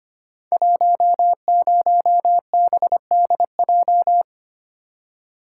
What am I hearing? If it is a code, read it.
Morse code, heard as a keyed tone, about 25 wpm, 700 Hz. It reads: 10BDJ
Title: Code Groups - 5 Letter:Numbers - 25wpm